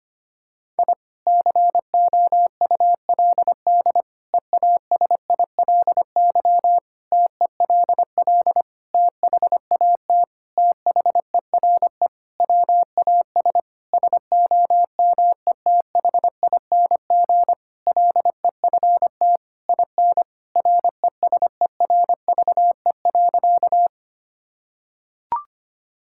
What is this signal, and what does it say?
Morse code, heard as a keyed tone, about 25 wpm, 700 Hz. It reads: I COULD EASILY TELL THAT THERE WAS SOMETHING LEFT IN RESERVE.
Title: Sentences from Top 1000 Words - Review Part 6 - 25wpm